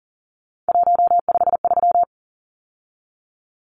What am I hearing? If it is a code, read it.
Morse code, heard as a keyed tone, about 37 wpm, 700 Hz. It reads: 153
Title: Numbers - 3-Digit 40wpm